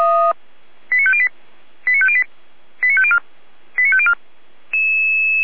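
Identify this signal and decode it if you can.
Five-tone selective call (ZVEI), heard as quick runs of five tones, tones of about 70 ms. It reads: 89508 89508 89504 89504
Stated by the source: ZVEI five-tone selective call received with an SDR, recorded for the Feuerwehr Orlamünde fire brigade.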